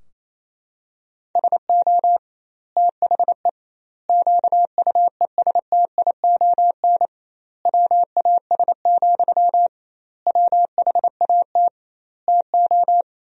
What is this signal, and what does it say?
Morse code, heard as a keyed tone, about 28 wpm, 700 Hz. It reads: SO THE QUESTION WAS, WHAT TO
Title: 2x Word-Spacing - Sentences from Top 400 Words - 28wpm